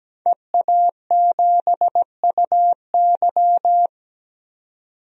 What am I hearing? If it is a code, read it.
Morse code, heard as a keyed tone, about 17 wpm, 700 Hz. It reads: EA7UY
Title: DXCC-100 Call Signs - 5 Characters - 17wpm